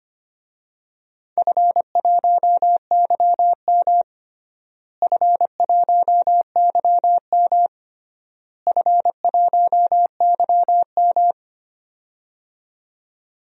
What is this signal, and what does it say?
Morse code, heard as a keyed tone, about 25 wpm, 700 Hz. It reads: F1YM F1YM F1YM
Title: DXCC-100 Call Signs - 4 Characters - Repeated 3x - 25wpm